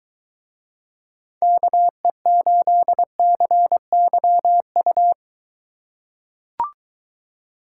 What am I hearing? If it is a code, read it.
Morse code, heard as a keyed tone, about 21 wpm, 700 Hz. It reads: KE8CYU